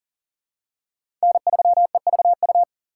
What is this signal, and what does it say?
Morse code, heard as a keyed tone, about 40 wpm, 700 Hz. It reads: N3EVU
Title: USA Call Signs - 5 Characters - 2x Character Spacing - 40wpm